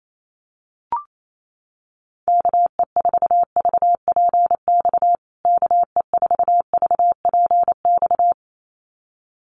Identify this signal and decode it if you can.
Morse code, heard as a keyed tone, about 28 wpm, 700 Hz. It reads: KE4VPX KE4VPX
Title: USA Call Signs Repeated 2x 28wpm